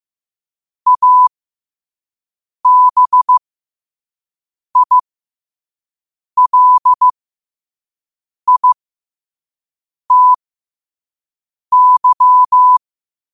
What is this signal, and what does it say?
Morse code, heard as a keyed tone, about 15 wpm, spaced as for 5 wpm, 1000 Hz. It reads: ABILITY
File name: qotdfiles / 5wpm